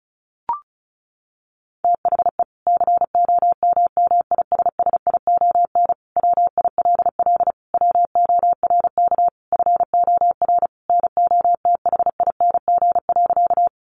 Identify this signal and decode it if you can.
Morse code, heard as a keyed tone, about 34 wpm, 700 Hz. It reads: THE COMMISSION WILL WORK FOR NOTHING.
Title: Sentences from Top 1400 Words 35wpm